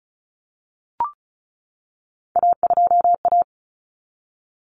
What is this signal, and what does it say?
Morse code, heard as a keyed tone, about 34 wpm, 700 Hz. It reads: A2A
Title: Common Cut Numbers - 3-Digit 35wpm